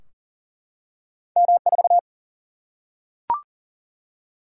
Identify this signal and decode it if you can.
Morse code, heard as a keyed tone, about 38 wpm, 700 Hz. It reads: M4
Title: Single-Letter Single-Number 40wpm